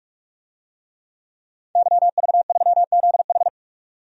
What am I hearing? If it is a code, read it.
Morse code, heard as a keyed tone, about 46 wpm, 700 Hz. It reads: YV3ZH
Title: DXCC-100 Call Signs - 5 Characters - 45wpm